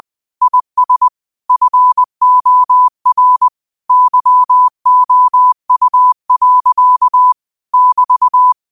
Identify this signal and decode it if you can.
Morse code, heard as a keyed tone, about 20 wpm, 1000 Hz. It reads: IS FOR YOU. <BT>
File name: qotdfiles / 20wpm